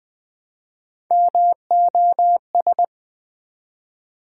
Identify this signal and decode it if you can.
Morse code, heard as a keyed tone, about 20 wpm, 700 Hz. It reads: MOS
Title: Build A Word - 1 Letter At A Time - 4-Letter Words - 20wpm